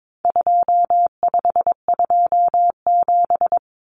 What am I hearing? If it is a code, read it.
Morse code, heard as a keyed tone, about 22 wpm, 700 Hz. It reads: 2527
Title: Numbers - 4-Digit 22wpm